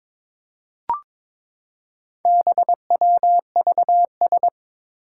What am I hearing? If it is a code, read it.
Morse code, heard as a keyed tone, about 22 wpm, 700 Hz. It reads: BWVS
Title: Four Challenging Letter-Number Combinations 22wpm